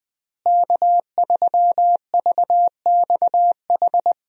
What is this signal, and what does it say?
Morse code, heard as a keyed tone, about 20 wpm, 700 Hz. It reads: K3VXH
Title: USA Call Signs - 5 Characters - Repeated 2x - 20wpm